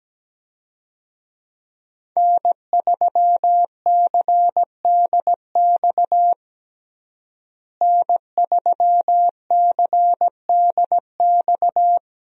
Morse code, heard as a keyed tone, about 17 wpm, 700 Hz. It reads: N3CDX N3CDX
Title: USA Call Signs - 5 Characters - Repeated 2x - 17wpm